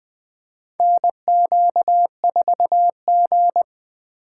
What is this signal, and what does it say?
Morse code, heard as a keyed tone, about 20 wpm, 700 Hz. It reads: NQ4G
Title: USA Call Signs - 2x Character Spacing - 20wpm